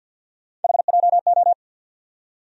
Morse code, heard as a keyed tone, about 49 wpm, 700 Hz. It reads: HJO
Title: USA Call Sign Postfixes - 3 Letters 50wpm